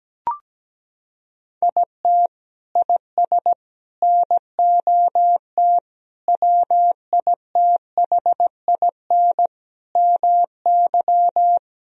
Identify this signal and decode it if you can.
Morse code, heard as a keyed tone, about 17 wpm, 700 Hz. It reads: IT IS NOT WITHIN MY
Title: Sentences from Top 1000 Words - Review Part 4 - 17wpm